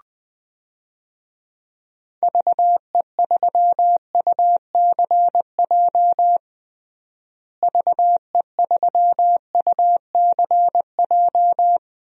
Morse code, heard as a keyed tone, about 20 wpm, 700 Hz. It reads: VE3UCJ VE3UCJ
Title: DXCC-100 Call Signs - 6 Characters - Repeated 3x - 20wpm